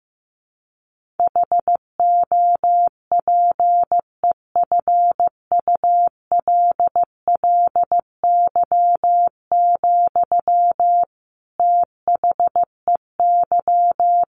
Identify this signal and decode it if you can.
Morse code, heard as a keyed tone, about 15 wpm, 700 Hz. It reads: HOPEFULLY, THEY
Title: Sentences from Top 1500 Words 15wpm